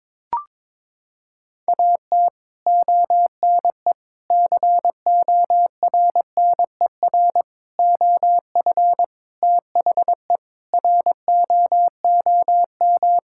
Morse code, heard as a keyed tone, about 22 wpm, 700 Hz. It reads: AT ONE CORNER OF THE ROOM
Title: Sentences from Top 1600 Words 22wpm